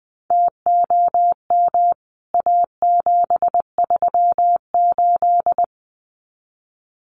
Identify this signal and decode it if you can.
Morse code, heard as a keyed tone, about 20 wpm, 700 Hz. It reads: TOM A738
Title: CWT Exchange 20wpm